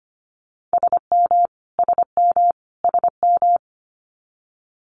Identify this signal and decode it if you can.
Morse code, heard as a keyed tone, about 25 wpm, 700 Hz. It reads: SM SM SM
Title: Multi-letter Phonemes Repeated 3x 25wpm